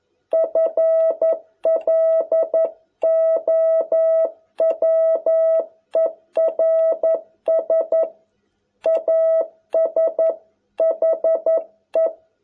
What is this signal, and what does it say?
Morse code, heard as a keyed tone, about 11 wpm, 650 Hz. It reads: FLOWERS AS HE